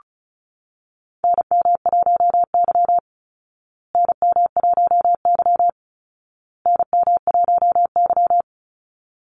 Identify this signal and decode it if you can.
Morse code, heard as a keyed tone, about 34 wpm, 700 Hz. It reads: NM1Y NM1Y NM1Y
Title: Call Signs Repeated 2x - 35wpm